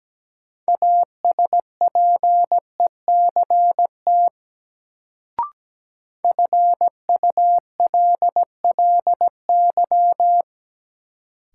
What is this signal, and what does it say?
Morse code, heard as a keyed tone, about 17 wpm, 700 Hz. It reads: ASPECT FULLY
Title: Sentences from Top 1300 Words - New Word Drill Alt - 17wpm